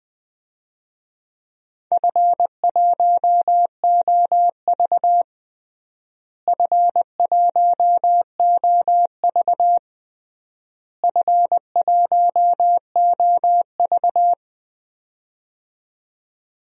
Morse code, heard as a keyed tone, about 19 wpm, 700 Hz. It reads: F1OV F1OV F1OV
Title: DXCC-100 Call Signs - 4 Characters - Repeated 3x - 20wpm